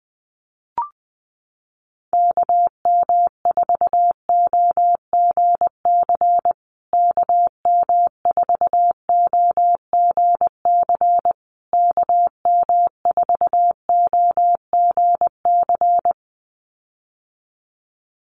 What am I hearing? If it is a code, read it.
Morse code, heard as a keyed tone, about 20 wpm, 700 Hz. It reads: KM4OGC KM4OGC KM4OGC
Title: USA Call Signs Repeated 3x 20wpm